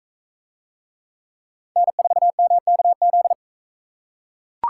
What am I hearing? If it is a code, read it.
Morse code, heard as a keyed tone, about 43 wpm, 700 Hz. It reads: N4MKZ